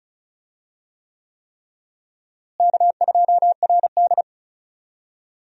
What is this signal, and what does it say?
Morse code, heard as a keyed tone, about 36 wpm, 700 Hz. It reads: K2RD
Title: USA Call Signs - 4 Characters - Repeated 3x - 35wpm